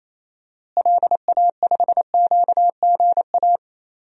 Morse code, heard as a keyed tone, about 28 wpm, 700 Hz. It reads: LA5QGA
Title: DXCC-100 Call Signs - 6 Characters - Repeated 2x - 28wpm